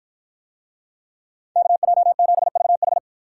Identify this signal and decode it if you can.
Morse code, heard as a keyed tone, about 49 wpm, 700 Hz. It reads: KJ7VH